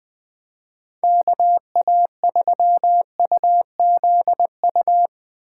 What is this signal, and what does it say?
Morse code, heard as a keyed tone, about 21 wpm, 700 Hz. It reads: KA3UZU
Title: Speed Racing - USA 2x3 Call Signs 17wpm